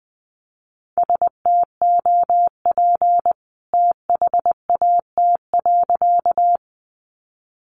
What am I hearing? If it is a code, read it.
Morse code, heard as a keyed tone, about 20 wpm, 700 Hz. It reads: STOP THAT.
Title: A Sound Of Thunder 20wpm